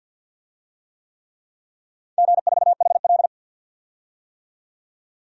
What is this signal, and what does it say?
Morse code, heard as a keyed tone, about 50 wpm, 700 Hz. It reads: G4HL
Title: DXCC-100 Call Signs - 4 Characters - 50wpm